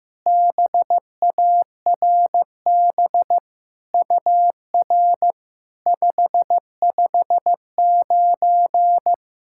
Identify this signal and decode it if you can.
Morse code, heard as a keyed tone, about 15 wpm, 700 Hz. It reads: BARB UR 559
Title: SOTA QSO Practice 15wpm